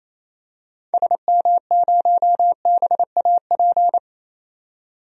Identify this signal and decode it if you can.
Morse code, heard as a keyed tone, about 28 wpm, 700 Hz. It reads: SM0BAP
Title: DXCC-100 Call Signs - 6 Characters - 28wpm